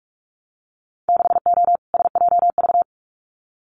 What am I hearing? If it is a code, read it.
Morse code, heard as a keyed tone, about 46 wpm, 700 Hz. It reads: 6O SJV
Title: Winter Field Day Exchanges Encore 45wpm